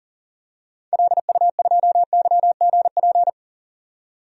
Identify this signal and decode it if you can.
Morse code, heard as a keyed tone, about 40 wpm, 700 Hz. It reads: LU2YGP
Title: DXCC-100 Call Signs - 6 Characters - 40wpm